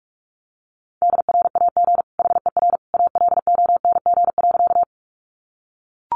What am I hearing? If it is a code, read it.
Morse code, heard as a keyed tone, about 44 wpm, 700 Hz. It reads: DRAG HER ALONG.
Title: Sentences from Top 1500 Words 45wpm